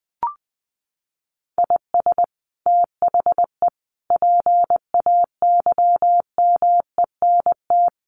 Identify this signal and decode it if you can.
Morse code, heard as a keyed tone, about 20 wpm, 700 Hz. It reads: IS THE PAYMENT
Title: Sentences from Top 1500 Words 20wpm